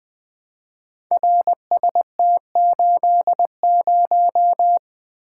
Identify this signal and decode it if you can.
Morse code, heard as a keyed tone, about 20 wpm, 700 Hz. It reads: RST80
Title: Code Groups - 5 Letter:Numbers - 20wpm